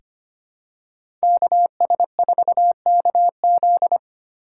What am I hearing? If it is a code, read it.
Morse code, heard as a keyed tone, about 25 wpm, 700 Hz. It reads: KS4KZ
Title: USA Call Signs - 5 Characters - 2x Character Spacing - 25wpm